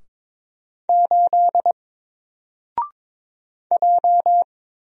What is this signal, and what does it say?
Morse code, heard as a keyed tone, about 22 wpm, 700 Hz. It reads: 8 J
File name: Single-Letter-Number-22wpm